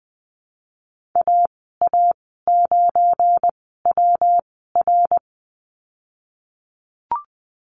Morse code, heard as a keyed tone, about 20 wpm, spaced as for 15 wpm, 700 Hz. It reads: AA9WR
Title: 173 Lesson 28g - Number 9 - Call Signs 20wpm